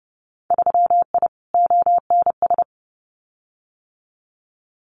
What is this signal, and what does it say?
Morse code, heard as a keyed tone, about 30 wpm, 700 Hz. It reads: 3I ONS
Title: Winter Field Day Exchanges Encore 30wpm